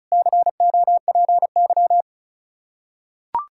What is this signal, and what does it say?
Morse code, heard as a keyed tone, about 34 wpm, 700 Hz. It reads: COPY
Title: Practice QSO - Exchange 3 - More Info - 35wpm